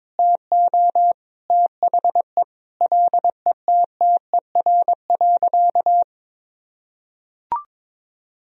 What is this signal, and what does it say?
Morse code, heard as a keyed tone, about 22 wpm, 700 Hz. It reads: TO THE LETTER.
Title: Sentences from Top 1000 Words - Review Part 1 - 22wpm